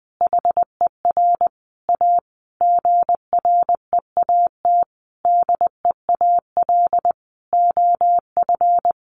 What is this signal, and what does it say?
Morse code, heard as a keyed tone, about 20 wpm, 700 Hz. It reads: HER A GREAT DEAL OF
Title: Sentences from Top 900 Words 20wpm